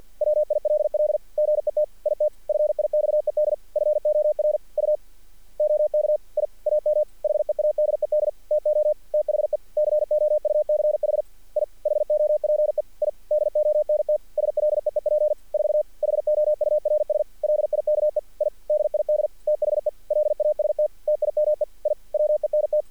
Morse code, heard as a keyed tone, about 46 wpm, 600 Hz. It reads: WILL GET IT FIXED FOR U OK I AM HEADED TO THE COUCH I HOPE I DONT SLEEP 4 HOURS LIKE I DID THE LAST TIME I WENT
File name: archived / CRSnet